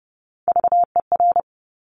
Morse code, heard as a keyed tone, about 30 wpm, 700 Hz. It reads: VER
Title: Top Three Letter Combinations in Words 30wpm